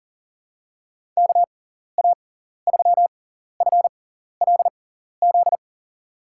Morse code, heard as a keyed tone, about 39 wpm, spaced as for 14 wpm, 700 Hz. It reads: KA3FLZ
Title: USA Call Signs - 6 Characters - 6x Character Spacing - 40wpm